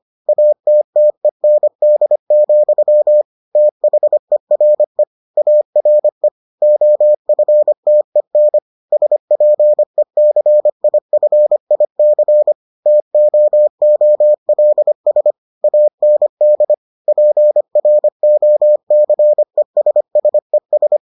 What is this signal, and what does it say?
Morse code, heard as a keyed tone, about 25 wpm, 600 Hz. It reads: ATTEND, THERE ARE OFTEN SPECIFIC TOOLS AND PROCESSES